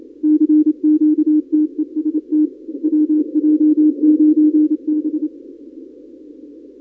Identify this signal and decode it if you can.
Morse code, heard as a keyed tone, about 27 wpm, 300 Hz. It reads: CQ TEST WJ9B
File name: contests / 2019-cw-wpx-kt5j-k5pi- -2330 0-31